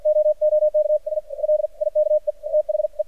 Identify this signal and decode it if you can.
Morse code, heard as a keyed tone, about 52 wpm, spaced as for 44 wpm, 610 Hz. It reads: OOK I LIKE THE
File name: archived / CRSnet